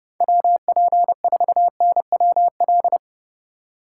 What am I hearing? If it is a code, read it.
Morse code, heard as a keyed tone, about 31 wpm, 700 Hz. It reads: WP4NWL